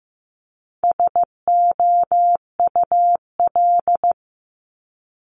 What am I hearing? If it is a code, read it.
Morse code, heard as a keyed tone, about 15 wpm, 700 Hz. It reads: SOUL
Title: Multi-letter Phonemes Repeated 3x 15wpm